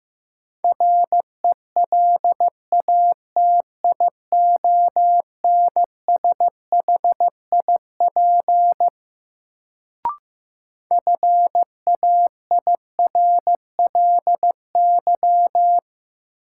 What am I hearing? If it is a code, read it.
Morse code, heard as a keyed tone, about 15 wpm, 700 Hz. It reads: RELATIONSHIP FAIRLY
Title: Sentences from Top 1100 Words - New Word Drill - 15wpm